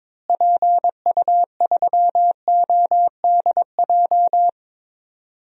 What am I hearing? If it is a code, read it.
Morse code, heard as a keyed tone, about 22 wpm, 700 Hz. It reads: PU3ODJ
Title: DXCC-100 Call Signs - 6 Characters - Repeated 3x - 22wpm